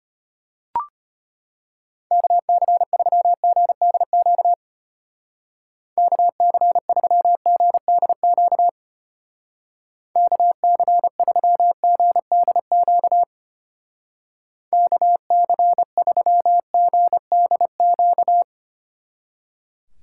Morse code, heard as a keyed tone, about 32 wpm, 700 Hz. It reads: KC3GDQ KC3GDQ KC3GDQ KC3GDQ